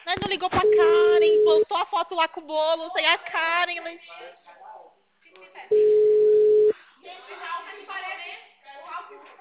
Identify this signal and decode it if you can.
Ringback tone (Europe, 425 Hz), with speech.